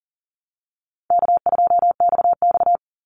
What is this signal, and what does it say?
Morse code, heard as a keyed tone, about 40 wpm, 700 Hz. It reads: K2XX
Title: Call Signs Repeated 2x - 40wpm